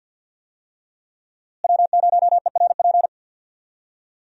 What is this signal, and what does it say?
Morse code, heard as a keyed tone, about 50 wpm, 700 Hz. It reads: W0ERP
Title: USA Call Signs - 5 Characters - Repeated 2x - 50wpm